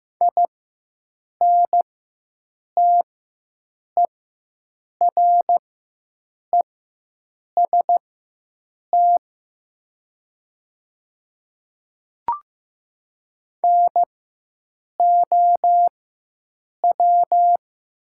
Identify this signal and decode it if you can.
Morse code, heard as a keyed tone, about 15 wpm, spaced as for 7 wpm, 700 Hz. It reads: INTEREST NOW
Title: 4x Character Spacing - Top 500 Words - 15wpm